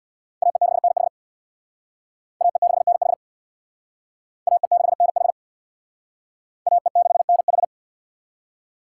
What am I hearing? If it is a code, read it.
Morse code, heard as a keyed tone, about 63 wpm, 700 Hz. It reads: AE6NH AE6NH AE6NH AE6NH